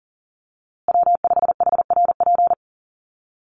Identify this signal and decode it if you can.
Morse code, heard as a keyed tone, about 40 wpm, 700 Hz. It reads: W5HRP
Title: USA Call Signs Repeated 2x 40wpm